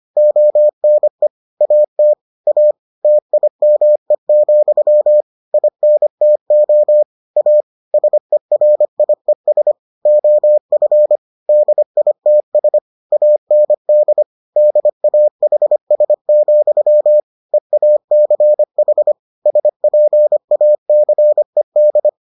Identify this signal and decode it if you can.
Morse code, heard as a keyed tone, about 25 wpm, 600 Hz. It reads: ONE AT A TIME, INTO A SERIES OF DITS AND DAHS, EACH SPACED